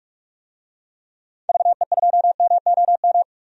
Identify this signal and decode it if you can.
Morse code, heard as a keyed tone, about 42 wpm, 700 Hz. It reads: VE2MOM
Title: DXCC-100 Call Signs - 6 Characters - 45wpm